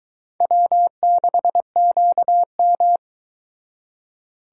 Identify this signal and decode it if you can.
Morse code, heard as a keyed tone, about 23 wpm, 700 Hz. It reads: W6QM